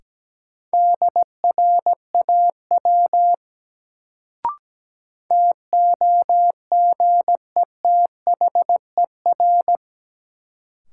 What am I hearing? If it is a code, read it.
Morse code, heard as a keyed tone, about 17 wpm, 700 Hz. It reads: DRAW TOGETHER
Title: Sentences from Top 300 Words - New Word Drill - 17wpm